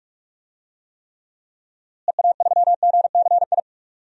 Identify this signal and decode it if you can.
Morse code, heard as a keyed tone, about 46 wpm, 700 Hz. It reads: EA3GCI
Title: DXCC-100 Call Signs - 6 Characters - Repeated 2x - 45wpm